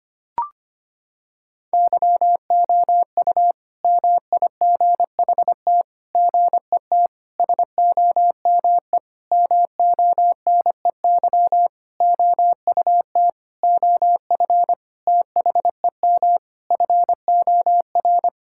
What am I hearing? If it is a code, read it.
Morse code, heard as a keyed tone, about 25 wpm, 700 Hz. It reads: YOU MIGHT GET SOME MONEY OUT OF THEM FOR
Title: Sentences from Top 1000 Words - Review Part 5 - 25wpm